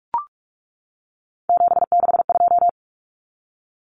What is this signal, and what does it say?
Morse code, heard as a keyed tone, about 44 wpm, 700 Hz. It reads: 762
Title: Common Cut Numbers - 3-Digit 45wpm